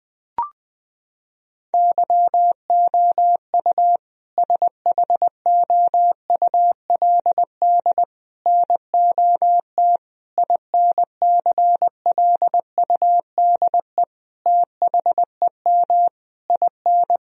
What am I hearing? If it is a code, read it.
Morse code, heard as a keyed tone, about 20 wpm, 700 Hz. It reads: YOU SHOULD NOT INCLUDE THEM IN
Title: Sentences from Top 1000 Words - Review Part 3 - 20wpm